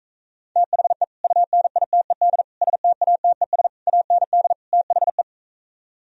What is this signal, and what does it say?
Morse code, heard as a keyed tone, about 46 wpm, 700 Hz. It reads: THE UNITED STATES AND THE